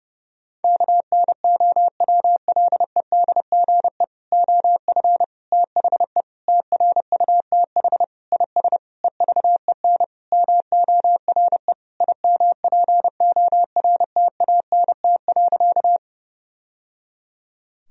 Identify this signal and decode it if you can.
Morse code, heard as a keyed tone, about 30 wpm, 700 Hz. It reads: KNOWLEDGE OF THE TRUTH IS EVEN MORE IMPORTANT.
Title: Sentences from Top 1000 Words - Review Part 3 - 30wpm